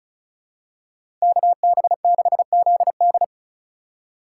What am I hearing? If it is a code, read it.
Morse code, heard as a keyed tone, about 34 wpm, 700 Hz. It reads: KB6ZD
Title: USA Call Signs - 5 Characters - 4x Character Spacing - 35wpm